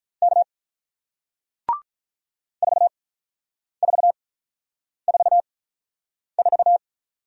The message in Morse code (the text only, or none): K 4444